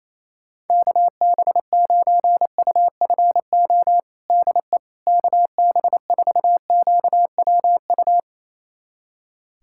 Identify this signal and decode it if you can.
Morse code, heard as a keyed tone, about 28 wpm, 700 Hz. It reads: KB9UFO DE KB4QWU
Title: USA Callsign DE Callsign - 6 Characters - 28wpm